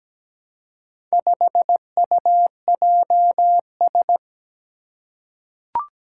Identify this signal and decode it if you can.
Morse code, heard as a keyed tone, about 17 wpm, 700 Hz. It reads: 5UJS
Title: Four Challenging Letter-Number Combinations 17wpm